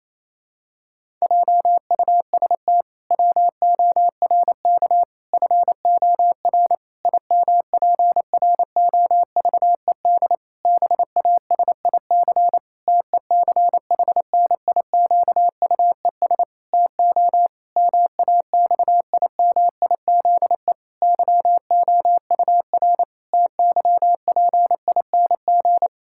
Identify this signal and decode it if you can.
Morse code, heard as a keyed tone, about 28 wpm, 700 Hz. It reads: JUST WORK FOR IMPROVED BASIC TECHNIQUES TO MAXIMIZE YOUR TYPING
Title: Pangram Challenge v2 - 28wpm